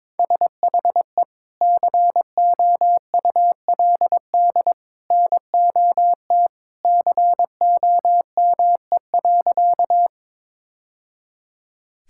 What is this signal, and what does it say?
Morse code, heard as a keyed tone, about 22 wpm, 700 Hz. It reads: SHE COULD NOT COME.
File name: practice / 22wpm / Sentences with 2-Word Phrases - 22wpm